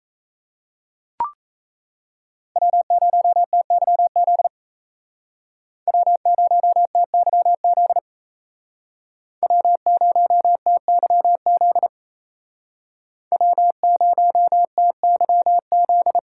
Morse code, heard as a keyed tone, about 34 wpm, 700 Hz. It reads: W0TYZ W0TYZ W0TYZ W0TYZ
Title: Speed Racing - USA 1x3 Call Signs 28wpm